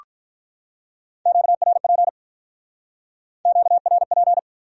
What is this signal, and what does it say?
Morse code, heard as a keyed tone, about 45 wpm, 700 Hz. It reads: QRP QRP